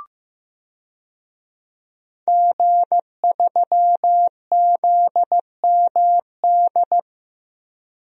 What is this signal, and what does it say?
Morse code, heard as a keyed tone, about 15 wpm, 700 Hz. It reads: G3ZMD
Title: DXCC-100 Call Signs - 5 Characters - Repeated 2x - 15wpm